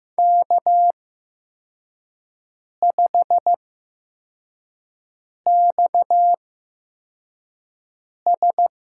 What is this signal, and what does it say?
Morse code, heard as a keyed tone, about 15 wpm, spaced as for 4 wpm, 700 Hz. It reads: K5XS
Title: USA Call Signs - 4 Characters - 8x Character Spacing - 15wpm